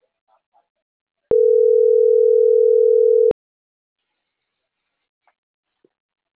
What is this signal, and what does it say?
Ringback tone (North America), 440 + 480 Hz.